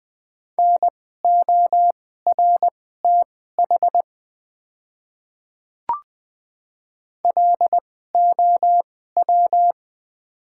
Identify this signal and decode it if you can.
Morse code, heard as a keyed tone, about 20 wpm, spaced as for 14 wpm, 700 Hz. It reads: NORTH LOW
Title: 2x Character Spacing - Top 300 Words - 20wpm